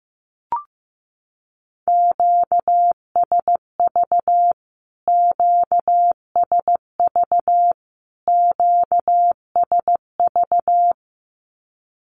Morse code, heard as a keyed tone, about 14 wpm, 700 Hz. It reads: QSV QSV QSV
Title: Q-Codes Repeated 3x 15wpm